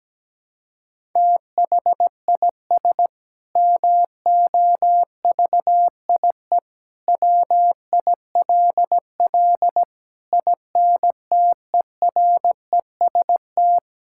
Morse code, heard as a keyed tone, about 17 wpm, 700 Hz. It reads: THIS MOVIE WILL INTEREST
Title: Sentences from Top 1000 Words - Review Part 5 - 17wpm